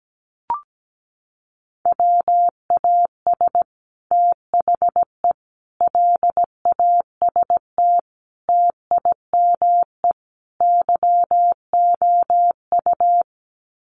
Morse code, heard as a keyed tone, about 17 wpm, 700 Hz. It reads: WAS THE LAST TIME YOU
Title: Sets of 5 Words Encore2 17wpm